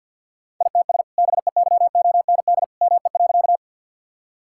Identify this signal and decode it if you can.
Morse code, heard as a keyed tone, about 50 wpm, 700 Hz. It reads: ITS BEYOND ME.
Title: Sentences from Top 1000 Words - Review Part 6 - 50wpm